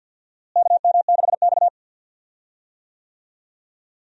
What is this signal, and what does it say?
Morse code, heard as a keyed tone, about 50 wpm, 700 Hz. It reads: KM6X
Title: USA Call Signs - 4 Characters - Repeated 3x - 50wpm